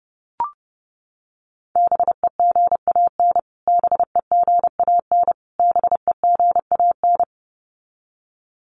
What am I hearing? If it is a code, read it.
Morse code, heard as a keyed tone, about 30 wpm, 700 Hz. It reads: BEGAN BEGAN BEGAN
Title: Top 500 Words Repeated 30wpm